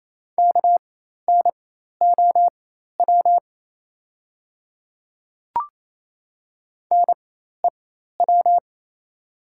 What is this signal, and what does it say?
Morse code, heard as a keyed tone, about 27 wpm, spaced as for 13 wpm, 700 Hz. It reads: KNOW NEW
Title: 4x Character Spacing - Top 300 Words - 28wpm